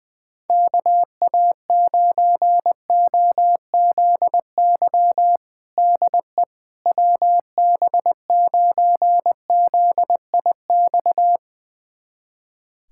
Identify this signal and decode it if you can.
Morse code, heard as a keyed tone, about 21 wpm, 700 Hz. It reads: KA9OZY DE WB9ZIX
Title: USA Callsign DE Callsign - 6 Characters - 20wpm